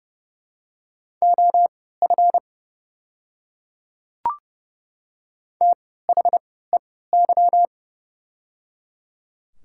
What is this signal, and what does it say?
Morse code, heard as a keyed tone, about 30 wpm, spaced as for 17 wpm, 700 Hz. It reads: OF THEY
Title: 3x Character Spacing - Top 200 Words - 30wpm